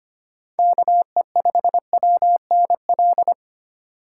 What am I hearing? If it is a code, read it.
Morse code, heard as a keyed tone, about 30 wpm, 700 Hz. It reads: KE5WNL
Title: Speed Racing - USA Call Signs 25wpm